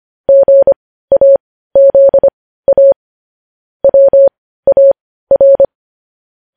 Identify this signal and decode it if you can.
Morse code, heard as a keyed tone, about 25 wpm, spaced as for 15 wpm, 550 Hz. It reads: GAZA WAR
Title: News Headlines at 15